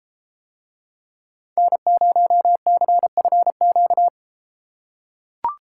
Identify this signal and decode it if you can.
Morse code, heard as a keyed tone, about 32 wpm, 700 Hz. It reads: N0CFQ